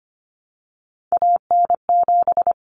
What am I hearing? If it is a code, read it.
Morse code, heard as a keyed tone, about 25 wpm, 700 Hz. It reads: AN7
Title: Common Cut Numbers - 3-Digit 25wpm